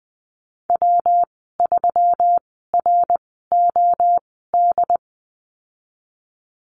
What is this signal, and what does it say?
Morse code, heard as a keyed tone, about 20 wpm, spaced as for 14 wpm, 700 Hz. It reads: W3ROD
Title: 122 Lesson 20g - Number 3 - Call Signs 20wpm